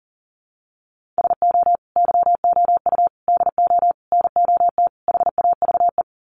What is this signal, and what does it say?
Morse code, heard as a keyed tone, about 40 wpm, 700 Hz. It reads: SO YOU DO NOT HAVE
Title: Sets of 5 Words Encore 40wpm